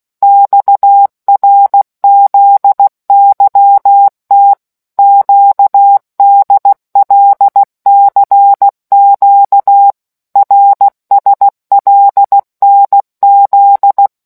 Morse code, heard as a keyed tone, about 16 wpm, 800 Hz. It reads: XRZYT QDLCQ RSLNZ